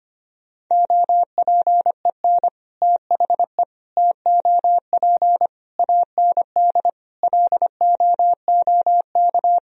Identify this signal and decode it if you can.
Morse code, heard as a keyed tone, about 25 wpm, 700 Hz. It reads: OPEN THE TOP AND LOOK
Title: Sentences from Top 1000 Words - Review Part 2 - 25wpm